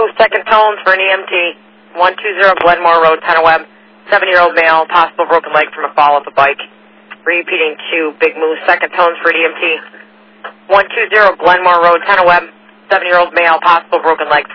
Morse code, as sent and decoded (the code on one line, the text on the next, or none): none